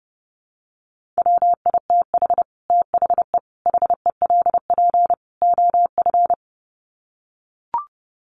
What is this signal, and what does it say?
Morse code, heard as a keyed tone, about 30 wpm, 700 Hz. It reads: WITH THE HELP OF
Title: Sets of 4 Words 30wpm